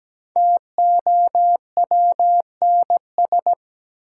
Morse code, heard as a keyed tone, about 17 wpm, 700 Hz. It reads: TOWNS
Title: Sentences from Top 1000 Words - Review Part 6 - New Word Drill - 17wpm